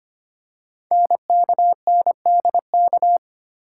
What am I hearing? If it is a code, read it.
Morse code, heard as a keyed tone, about 25 wpm, 700 Hz. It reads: NKNDK
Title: Code Groups - 5 Letters - 25wpm